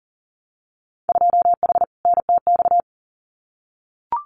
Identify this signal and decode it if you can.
Morse code, heard as a keyed tone, about 42 wpm, 700 Hz. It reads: 2H NTX
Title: Winter Field Day Exchanges Encore 40wpm